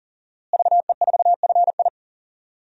Morse code, heard as a keyed tone, about 40 wpm, 700 Hz. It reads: VE4FI
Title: DXCC-100 Call Signs - 5 Characters - Repeated 3x - 40wpm